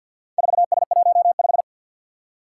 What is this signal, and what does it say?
Morse code, heard as a keyed tone, about 51 wpm, 700 Hz. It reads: 4S15
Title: Four Challenging Letter-Number Combinations 50wpm